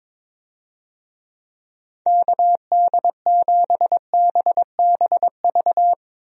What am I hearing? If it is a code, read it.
Morse code, heard as a keyed tone, about 22 wpm, 700 Hz. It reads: KD7BBV
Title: USA Call Signs - 6 Characters - Repeated 2x - 22wpm